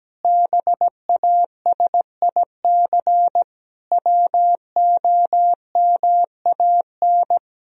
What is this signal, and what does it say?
Morse code, heard as a keyed tone, about 17 wpm, 700 Hz. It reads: BASIC WOMAN
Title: Sentences from Top 1000 Words - Review Part 3 - 17wpm